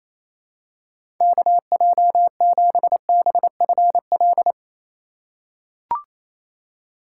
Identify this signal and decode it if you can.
Morse code, heard as a keyed tone, about 28 wpm, 700 Hz. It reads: KJ7BFL
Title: USA Call Signs - 4x Character Spacing - 28wpm